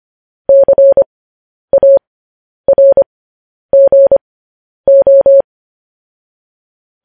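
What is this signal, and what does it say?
Morse code, heard as a keyed tone, about 25 wpm, spaced as for 10 wpm, 550 Hz. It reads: CARGO
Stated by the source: Morse code news headlines sent at 10 wpm.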